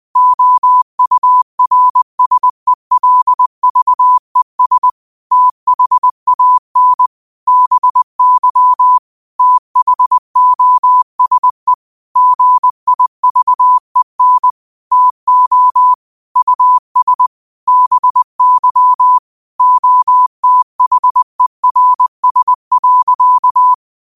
Morse code, heard as a keyed tone, about 20 wpm, 1000 Hz. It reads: OURSELVES THAN BY THOSE GIVEN TO US BY OTHERS.